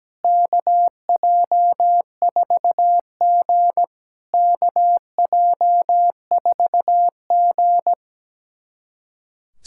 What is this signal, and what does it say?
Morse code, heard as a keyed tone, about 17 wpm, 700 Hz. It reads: KJ4G KJ4G
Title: SOTA QSO Practice 17wpm